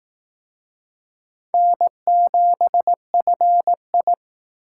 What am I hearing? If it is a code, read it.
Morse code, heard as a keyed tone, about 18 wpm, 700 Hz. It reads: N7FI
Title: Speed Racing - USA Call Signs 15wpm